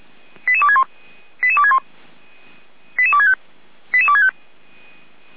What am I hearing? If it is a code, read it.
Five-tone selective call (ZVEI), heard as quick runs of five tones, tones of about 70 ms. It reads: 88261 88261 88265 88265